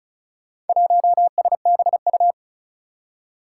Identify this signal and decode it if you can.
Morse code, heard as a keyed tone, about 36 wpm, 700 Hz. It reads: 1SBU
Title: Four Challenging Letter-Number Combinations 35wpm